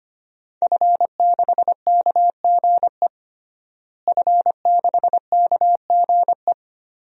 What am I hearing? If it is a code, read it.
Morse code, heard as a keyed tone, about 25 wpm, 700 Hz. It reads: F6KGE F6KGE
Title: DXCC-100 Call Signs - 5 Characters - Repeated 2x - 25wpm